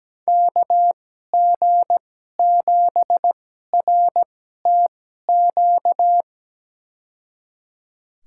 Morse code, heard as a keyed tone, about 17 wpm, spaced as for 12 wpm, 700 Hz. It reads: KG7RTQ
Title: USA Call Signs - 6 Characters - 2x Character Spacing - 17wpm